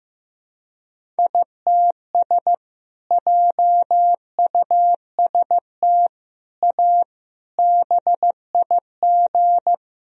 Morse code, heard as a keyed tone, about 15 wpm, 700 Hz. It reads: ITS JUST A BIG
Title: Sentences from Top 1000 Words - Review Part 1 - 15wpm